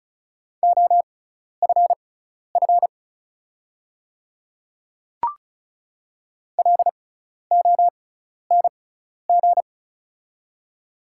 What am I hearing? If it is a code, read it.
Morse code, heard as a keyed tone, about 36 wpm, spaced as for 12 wpm, 690 Hz. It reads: OFF LONG
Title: 6x Character Spacing - Top 300 Words - 35wpm